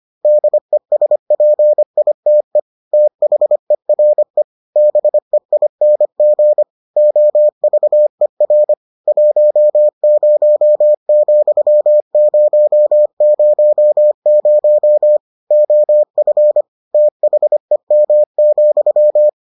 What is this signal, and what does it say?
Morse code, heard as a keyed tone, about 25 wpm, 600 Hz. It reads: DESPITE THERE BEING OVER 10,000 OF THEM,